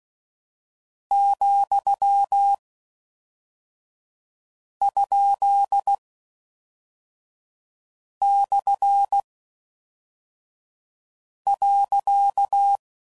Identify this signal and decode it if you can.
Morse code, heard as a keyed tone, about 16 wpm, spaced as for 4 wpm, 800 Hz. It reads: ,?/.